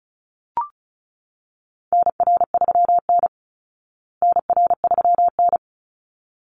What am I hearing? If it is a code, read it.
Morse code, heard as a keyed tone, about 34 wpm, 700 Hz. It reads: NR3N NR3N
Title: Call Signs Repeated 3x - 35wpm